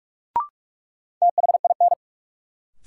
Morse code, heard as a keyed tone, about 46 wpm, 700 Hz. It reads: THIN
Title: Sentences from Top 1200 Words - New Word Drill Alt - 45wpm